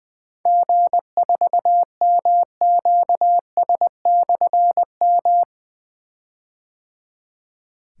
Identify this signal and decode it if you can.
Morse code, heard as a keyed tone, about 20 wpm, 700 Hz. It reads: G4MQS/M
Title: Difficult Call Signs 20wpm